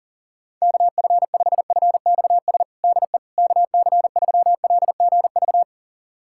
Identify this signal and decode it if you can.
Morse code, heard as a keyed tone, about 40 wpm, 700 Hz. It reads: KF5FXS DE KC3LGV
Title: USA Callsign DE Callsign - 6 Characters - 40wpm